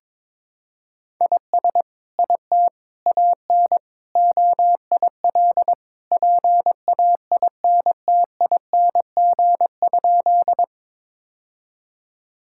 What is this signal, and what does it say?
Morse code, heard as a keyed tone, about 22 wpm, 700 Hz. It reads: IS IT AN OIL PAINTING?
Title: Sentences from Top 1000 Words - Review Part 6 - 22wpm